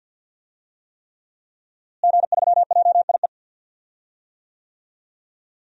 Morse code, heard as a keyed tone, about 53 wpm, 700 Hz. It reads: G3JIE
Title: DXCC-100 Call Signs - 5 Characters - Repeated 3x - 50wpm